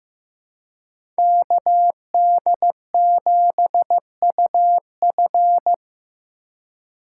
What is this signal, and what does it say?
Morse code, heard as a keyed tone, about 15 wpm, 700 Hz. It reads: KD7UF
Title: USA Call Signs - 4x Character Spacing - 15wpm